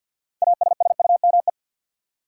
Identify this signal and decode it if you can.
Morse code, heard as a keyed tone, about 52 wpm, 700 Hz. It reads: ASSUME
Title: Sentences from Top 1500 Words - New Word Drill - 50wpm